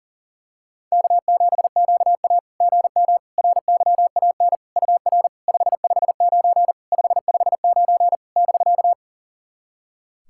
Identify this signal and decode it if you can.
Morse code, heard as a keyed tone, about 40 wpm, 690 Hz. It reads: K7QA GM RYAN UR 559 559 <BK>
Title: SOTA QSO Practice 40wpm